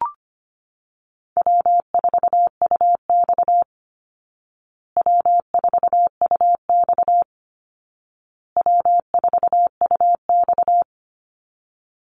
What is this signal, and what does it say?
Morse code, heard as a keyed tone, about 25 wpm, 700 Hz. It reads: W4UX W4UX W4UX
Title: Easy Call Signs Repeated 3x - 25wpm